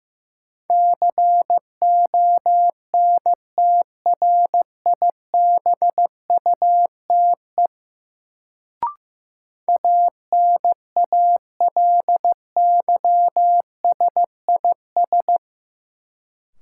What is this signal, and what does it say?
Morse code, heard as a keyed tone, about 15 wpm, 700 Hz. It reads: CONTRIBUTE ANALYSIS
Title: Sentences from Top 1400 Words - New Word Drill Alt - 15wpm